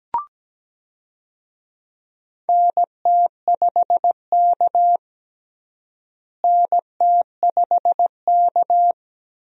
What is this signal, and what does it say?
Morse code, heard as a keyed tone, about 17 wpm, 700 Hz. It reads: NT5K NT5K
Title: USA Call Signs - 4 Characters - Repeated 2x - 17wpm